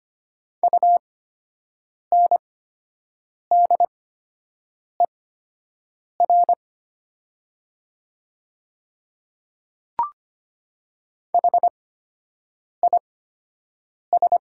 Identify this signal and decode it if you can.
Morse code, heard as a keyed tone, about 25 wpm, spaced as for 7 wpm, 700 Hz. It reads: UNDER HIS
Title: 8x Character Spacing - Top 200 Words - 25wpm